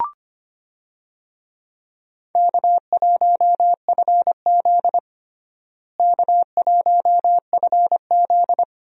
Morse code, heard as a keyed tone, about 25 wpm, 700 Hz. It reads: K1FZ K1FZ
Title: USA Call Signs - 4 Characters - Repeated 3x - 25wpm